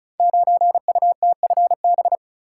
Morse code, heard as a keyed tone, about 36 wpm, 700 Hz. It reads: 9UTFB
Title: Code Groups - 5 Letter:Numbers - 35wpm